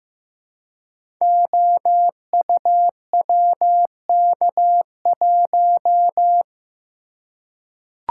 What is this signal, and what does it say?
Morse code, heard as a keyed tone, about 15 wpm, 700 Hz. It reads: OUWK1
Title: Code Groups - 5 Letter:Numbers - 15wpm